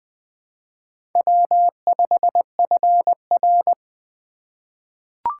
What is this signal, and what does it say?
Morse code, heard as a keyed tone, about 20 wpm, 700 Hz. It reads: W5FR
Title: USA Call Signs - 4 Characters - 3x Character Spacing - 20wpm